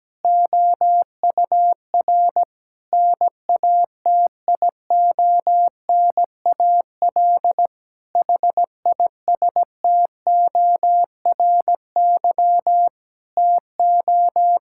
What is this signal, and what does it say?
Morse code, heard as a keyed tone, about 17 wpm, 700 Hz. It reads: OUR NATIONAL HISTORY TO
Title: Sentences from Top 1000 Words - Review Part 6 - 17wpm